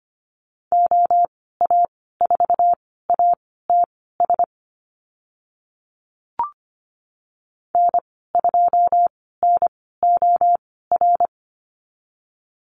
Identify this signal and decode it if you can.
Morse code, heard as a keyed tone, about 25 wpm, spaced as for 16 wpm, 690 Hz. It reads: OA4ATS N2NOR
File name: Lesson 15g - Number 5 - Call Signs 25wpm